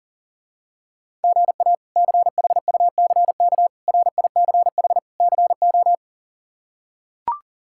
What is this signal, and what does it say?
Morse code, heard as a keyed tone, about 37 wpm, 700 Hz. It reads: GA CHUCK RICH CO
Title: K1USN SST QSO Practice - 40wpm